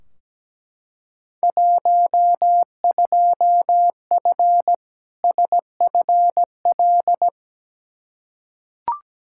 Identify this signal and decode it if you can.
Morse code, heard as a keyed tone, about 17 wpm, 700 Hz. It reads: 12F SFL
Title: Field Day Exchanges Encore - 17wpm